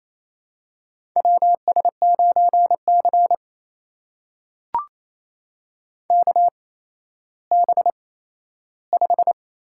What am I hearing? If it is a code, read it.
Morse code, heard as a keyed tone, about 28 wpm, spaced as for 14 wpm, 700 Hz. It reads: WS9C KB5
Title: USA Call Signs - 8x Character Spacing - 28wpm